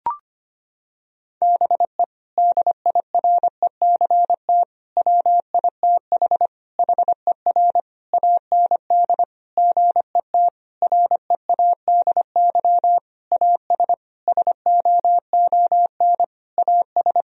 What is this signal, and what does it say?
Morse code, heard as a keyed tone, about 25 wpm, 700 Hz. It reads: BE DIRECT WITH HER AND GET READY AS SOON AS
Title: Sentences from Top 1000 Words - Review Part 2 - 25wpm